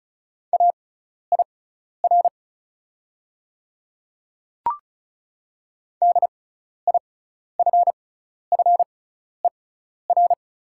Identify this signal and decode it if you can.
Morse code, heard as a keyed tone, about 34 wpm, spaced as for 12 wpm, 690 Hz. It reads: AIR DIFFER
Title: 6x Character Spacing - Top 200 Words - 35wpm